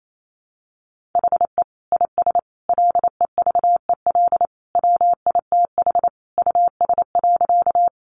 Morse code, heard as a keyed tone, about 28 wpm, 700 Hz. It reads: HE IS LEVEL WITH US.
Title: Sentences from Top 700 Words 28wpm